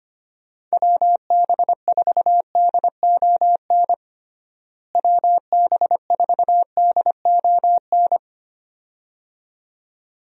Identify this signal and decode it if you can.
Morse code, heard as a keyed tone, about 25 wpm, 700 Hz. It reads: WB4DON WB4DON
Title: USA Call Signs - 6 Characters - Repeated 2x - 25wpm